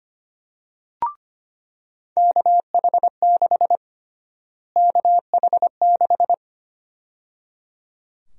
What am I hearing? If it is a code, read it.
Morse code, heard as a keyed tone, about 25 wpm, 700 Hz. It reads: KH6 KH6
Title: USA Call Sign Prefixes - Repeated 2x - 25wpm